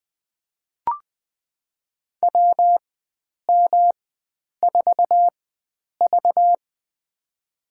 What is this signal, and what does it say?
Morse code, heard as a keyed tone, about 20 wpm, spaced as for 9 wpm, 700 Hz. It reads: WM4V
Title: USA Call Signs - 4 Characters - 4x Character Spacing - 20wpm